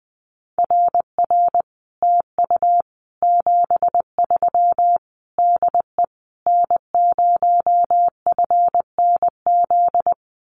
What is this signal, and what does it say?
Morse code, heard as a keyed tone, about 20 wpm, 700 Hz. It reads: RR TU 73 DE N0FNZ
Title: Common QSO Phrases Encore 20wpm